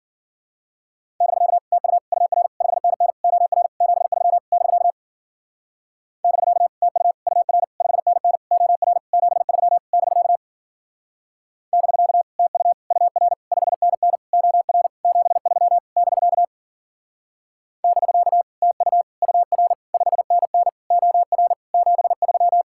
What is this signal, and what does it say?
Morse code, heard as a keyed tone, about 49 wpm, 700 Hz. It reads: <BK> TU UR 5NN OR 73 <BK> <BK> TU UR 5NN OR 73 <BK> <BK> TU UR 5NN OR 73 <BK> <BK> TU UR 5NN OR 73
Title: Speed-Racing Common QSO Phrases - 40wpm